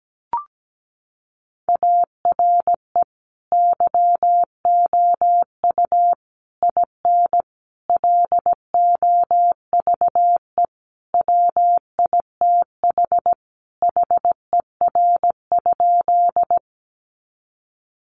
Morse code, heard as a keyed tone, about 17 wpm, 700 Hz. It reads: ARE YOU IN LOVE WITH HER?
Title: Sentences from Top 400 Words 17wpm